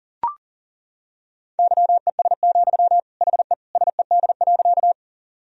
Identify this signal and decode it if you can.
Morse code, heard as a keyed tone, about 40 wpm, 700 Hz. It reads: YES, HE SED.
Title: Sentences from Top 500 Words and Abbreviations 40wpm